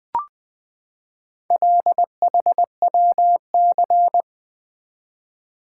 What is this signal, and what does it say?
Morse code, heard as a keyed tone, about 20 wpm, 700 Hz. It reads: LHWC
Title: Four Letter-Number Combinations - 20wpm